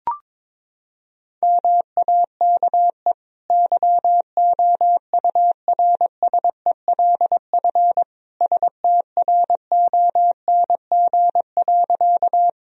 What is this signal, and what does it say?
Morse code, heard as a keyed tone, about 22 wpm, 700 Hz. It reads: MAKE YOURSELF STRONG.
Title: Sentences from Top 1000 Words - Review Part 3 - 22wpm